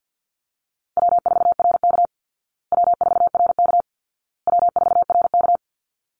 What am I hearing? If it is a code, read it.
Morse code, heard as a keyed tone, about 50 wpm, 700 Hz. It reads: W4RK W4RK W4RK
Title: Easy Call Signs Repeated 3x - 50wpm